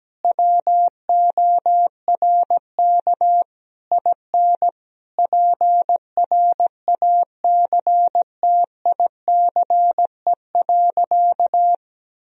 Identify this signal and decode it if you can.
Morse code, heard as a keyed tone, about 17 wpm, 700 Hz. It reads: WORK IN PRACTICE.
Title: Sentences from Top 1000 Words - Review Part 4 - 17wpm